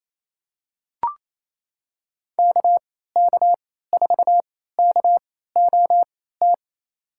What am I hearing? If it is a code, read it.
Morse code, heard as a keyed tone, about 28 wpm, spaced as for 16 wpm, 700 Hz. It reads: KK4KOT
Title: USA Call Signs - 6 Characters - 3x Character Spacing - 28wpm